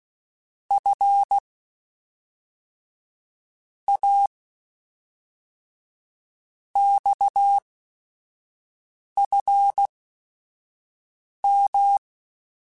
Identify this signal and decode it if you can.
Morse code, heard as a keyed tone, about 16 wpm, 800 Hz. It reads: F A X F M
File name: Technical Subjects / Technical Subjects-mp3s / Sw Pwr Sup @5